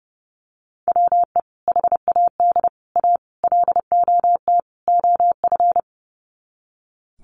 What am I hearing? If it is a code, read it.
Morse code, heard as a keyed tone, about 30 wpm, 700 Hz. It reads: WE HAD A LOT OF
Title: Sets of 5 Words Encore 30wpm